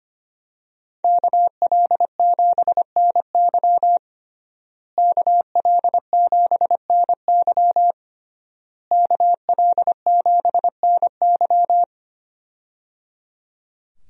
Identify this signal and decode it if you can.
Morse code, heard as a keyed tone, about 25 wpm, 700 Hz. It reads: KL7NY KL7NY KL7NY
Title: USA Call Signs - 5 Characters - Repeated 3x - 25wpm